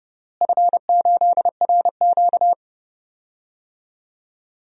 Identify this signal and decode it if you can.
Morse code, heard as a keyed tone, about 31 wpm, 700 Hz. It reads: F8RQ
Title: DXCC-100 Call Signs - 4 Characters - Repeated 3x - 30wpm